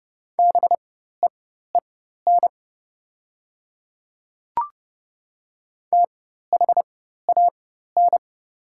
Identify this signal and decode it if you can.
Morse code, heard as a keyed tone, about 30 wpm, spaced as for 14 wpm, 700 Hz. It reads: BEEN THAN
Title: 4x Character Spacing - Top 300 Words - 30wpm